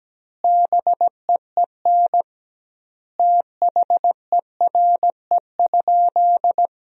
Morse code, heard as a keyed tone, about 17 wpm, 700 Hz. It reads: BEEN THERE?
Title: 2x Word-Spacing - Sentences from Top 300 Words - 17wpm